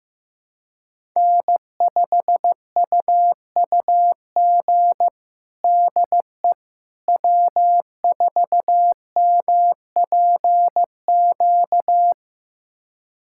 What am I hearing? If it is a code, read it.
Morse code, heard as a keyed tone, about 15 wpm, 700 Hz. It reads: N5UUG DE W4MPQ
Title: USA Callsign DE Callsign - 5 Characters - 15wpm